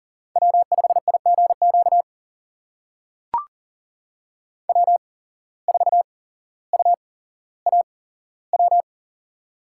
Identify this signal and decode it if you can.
Morse code, heard as a keyed tone, about 38 wpm, spaced as for 17 wpm, 700 Hz. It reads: W5IGQ W4UAW
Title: USA Call Signs - 8x Character Spacing - 40wpm